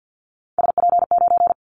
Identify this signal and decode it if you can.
Morse code, heard as a keyed tone, about 50 wpm, 700 Hz. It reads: SP9
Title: Callsign Prefixes for Top 25 DXCC - 50wpm